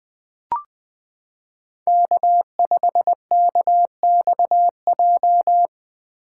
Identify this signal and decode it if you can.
Morse code, heard as a keyed tone, about 20 wpm, 700 Hz. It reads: K5KXJ
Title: K1USN SST QSO Practice - 20wpm